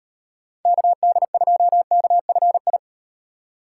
Morse code, heard as a keyed tone, about 36 wpm, 700 Hz. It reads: KD2KFI